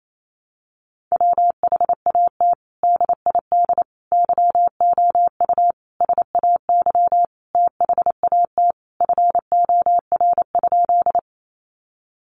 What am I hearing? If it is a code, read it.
Morse code, heard as a keyed tone, about 28 wpm, 700 Hz. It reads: WHAT DID YOU SAY THAT FOR?
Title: Sentences from Top 200 Words 28wpm